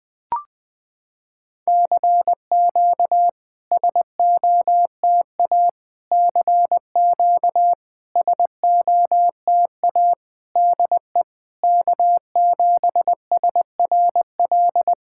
Morse code, heard as a keyed tone, about 20 wpm, 700 Hz. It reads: CQ SOTA CQ SOTA DE K7SRL
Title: SOTA QSO Practice 20wpm